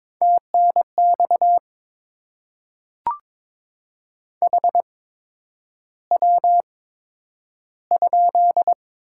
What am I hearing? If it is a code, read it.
Morse code, heard as a keyed tone, about 22 wpm, spaced as for 9 wpm, 700 Hz. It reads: TNX HW?
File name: Top 20 QSO Elements - 8x Character Spacing - 22wpm